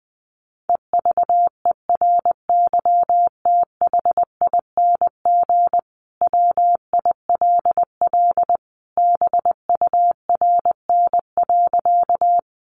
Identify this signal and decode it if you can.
Morse code, heard as a keyed tone, about 20 wpm, 700 Hz. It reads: EVERYTHING WILL BURN.
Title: Sentences from Top 1300 Words 20wpm